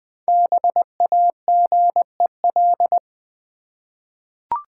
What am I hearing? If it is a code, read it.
Morse code, heard as a keyed tone, about 20 wpm, 700 Hz. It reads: BAGEL
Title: Easy Words with Hints - 20wpm